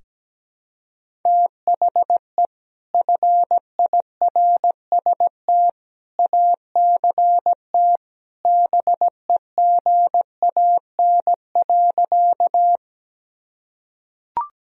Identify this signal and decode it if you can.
Morse code, heard as a keyed tone, about 17 wpm, 700 Hz. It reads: THE FIRST ACT BEGAN.
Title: Sentences from Top 1000 Words - Review Part 1 - 17wpm